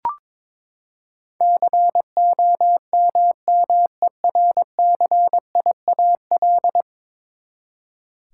Morse code, heard as a keyed tone, about 22 wpm, 700 Hz. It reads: COMMERCIAL
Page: morse-code-ninja-practice-22wpm